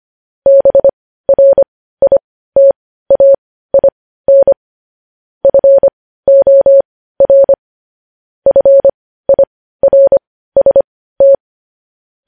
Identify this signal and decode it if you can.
Morse code, heard as a keyed tone, about 25 wpm, spaced as for 15 wpm, 550 Hz. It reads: BRITAIN FOR FIRST
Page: news-headlines-in-morse-code-at-15-wpm